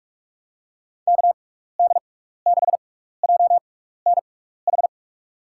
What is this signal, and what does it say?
Morse code, heard as a keyed tone, about 46 wpm, spaced as for 16 wpm, 700 Hz. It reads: KD6JNH
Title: USA Call Signs - 6 Characters - 6x Character Spacing - 45wpm